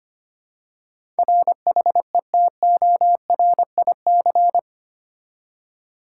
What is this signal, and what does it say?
Morse code, heard as a keyed tone, about 25 wpm, 700 Hz. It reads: RHETORIC
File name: Eight Letter Words - 25wpm